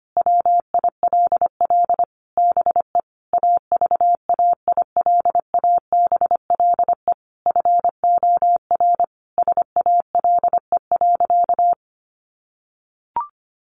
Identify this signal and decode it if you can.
Morse code, heard as a keyed tone, about 25 wpm, 700 Hz. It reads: WILL BE AVAILABLE FOR SALE.
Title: Sentences from Top 1400 Words 25wpm